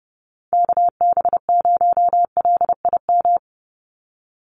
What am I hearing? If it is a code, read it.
Morse code, heard as a keyed tone, about 30 wpm, 700 Hz. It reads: KB0LIM
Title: USA Call Signs 30wpm